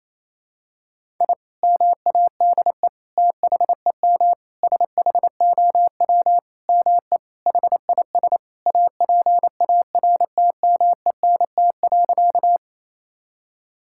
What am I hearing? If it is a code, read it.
Morse code, heard as a keyed tone, about 28 wpm, 700 Hz. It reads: I MADE THEM SHOW ME HIS APARTMENT.
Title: Sentences from Top 1000 Words - Review Part 6 - 28wpm